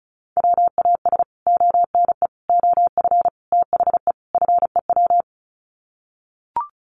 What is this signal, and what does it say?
Morse code, heard as a keyed tone, about 34 wpm, 700 Hz. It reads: WAS ONE OF THE FEW
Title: Sets of 5 Words Encore 35wpm